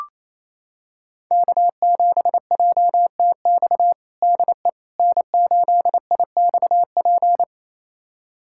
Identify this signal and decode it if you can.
Morse code, heard as a keyed tone, about 28 wpm, 690 Hz. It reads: K7JTX DE N8IXP
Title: USA Callsign DE Callsign - 5 Characters - 28wpm